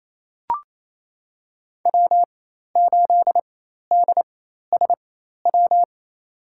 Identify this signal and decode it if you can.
Morse code, heard as a keyed tone, about 28 wpm, spaced as for 13 wpm, 700 Hz. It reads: W8DSW
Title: USA Call Signs - 4x Character Spacing - 28wpm